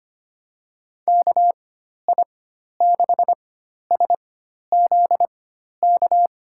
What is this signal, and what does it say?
Morse code, heard as a keyed tone, about 25 wpm, spaced as for 12 wpm, 700 Hz. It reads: KI6SZK
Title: USA Call Signs - 6 Characters - 4x Character Spacing - 25wpm